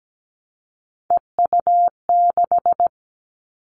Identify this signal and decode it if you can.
Morse code, heard as a keyed tone, about 17 wpm, 700 Hz. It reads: EU6
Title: Callsign Prefixes for Top 100 DXCC - 17wpm